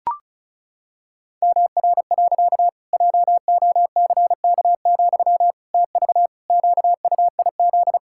Morse code, heard as a keyed tone, about 36 wpm, 700 Hz. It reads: MR. JOCK, TV QUIZ
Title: Pangram Challenge v2 - 35wpm